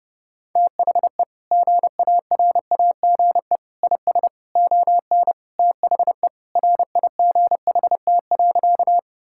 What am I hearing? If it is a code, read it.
Morse code, heard as a keyed tone, about 30 wpm, 700 Hz. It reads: THE GARAGE IS ON THE RIGHT.
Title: Sentences from Top 1600 Words 30wpm